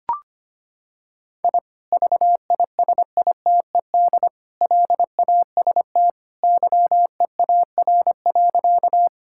Morse code, heard as a keyed tone, about 25 wpm, 700 Hz. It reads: I VISITED LAST YEAR.
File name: practice / 25wpm / Sentences with 2-Word Phrases - 25wpm